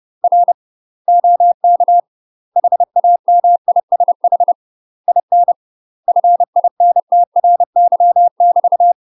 Morse code, heard as a keyed tone, about 30 wpm, 700 Hz. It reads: R OK HAMISH IN FINTRY <BT>
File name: UK MASTER QSOs TRAINING / 30WPM / UK MASTER QSOs 0017